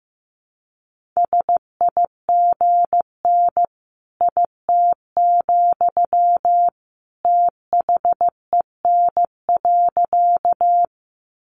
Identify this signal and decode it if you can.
Morse code, heard as a keyed tone, about 15 wpm, 700 Hz. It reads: SIGN IT, THEN.
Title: Sentences from Top 1000 Words 15wpm